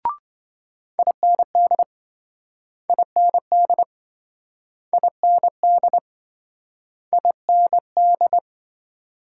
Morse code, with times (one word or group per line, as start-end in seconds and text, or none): 1.0-1.8 IND
2.9-3.8 IND
4.9-6.0 IND
7.1-8.4 IND